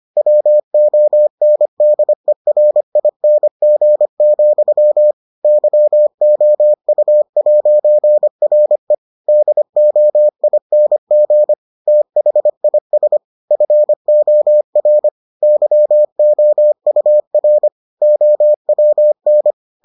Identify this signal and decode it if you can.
Morse code, heard as a keyed tone, about 25 wpm, 600 Hz. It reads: WONDERING, YOU'RE DOING THIS FOR YOUR OWN